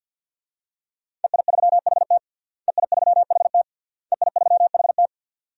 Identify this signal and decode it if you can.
Morse code, heard as a keyed tone, about 51 wpm, 700 Hz. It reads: EI3HT EI3HT EI3HT
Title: DXCC-100 Call Signs - 5 Characters - Repeated 3x - 50wpm